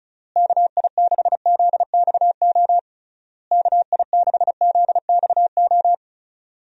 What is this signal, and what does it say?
Morse code, heard as a keyed tone, about 36 wpm, 700 Hz. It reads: KI6ZXO KI6ZXO
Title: USA Call Signs - 6 Characters - Repeated 3x - 35wpm